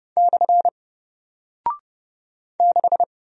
Morse code, heard as a keyed tone, about 30 wpm, 700 Hz. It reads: / 6
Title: Single Character (Letter:Number:Punctuation:Prosign) Alt 30wpm